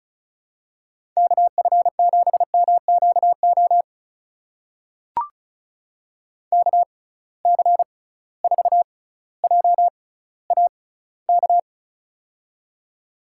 Morse code, heard as a keyed tone, about 36 wpm, spaced as for 18 wpm, 700 Hz. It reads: KF7MQO KC4JAK
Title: USA Call Signs - 6 Characters - 6x Character Spacing - 35wpm